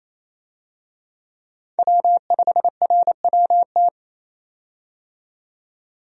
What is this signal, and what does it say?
Morse code, heard as a keyed tone, about 28 wpm, 700 Hz. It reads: W5RWT
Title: USA Call Signs - 5 Characters - 28wpm